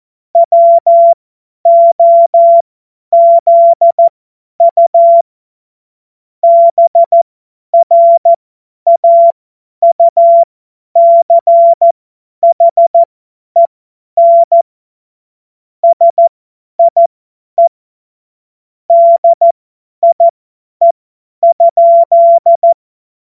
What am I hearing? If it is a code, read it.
Morse code, heard as a keyed tone, about 14 wpm, spaced as for 10 wpm, 680 Hz. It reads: WOZU BRAUCHEN SIE DIE?
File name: Morsekurs / Witze / Witz 28. März 2023 50 F70